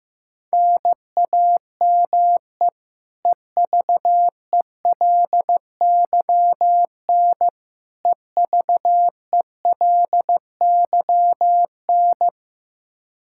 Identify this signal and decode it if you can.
Morse code, heard as a keyed tone, about 15 wpm, 700 Hz. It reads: NAME EVELYN EVELYN
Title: Practice QSO - Exchange 1 - Beginning - 15wpm